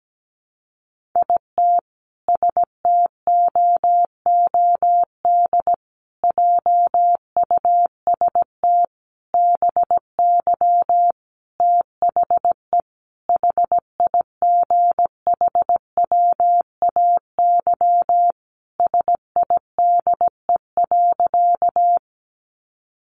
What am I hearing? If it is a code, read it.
Morse code, heard as a keyed tone, about 17 wpm, 700 Hz. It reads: IT STOOD JUST BY THE HIGHWAY SIDE.
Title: Sentences from Top 1400 Words 17wpm